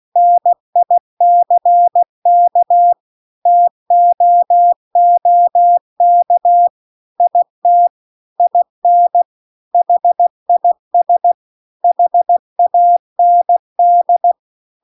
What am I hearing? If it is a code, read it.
Morse code, heard as a keyed tone, about 16 wpm, 700 Hz. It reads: NICK TOOK IT IN HIS HAND
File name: CW-The-Big-Twoheaded-River-2-16WPM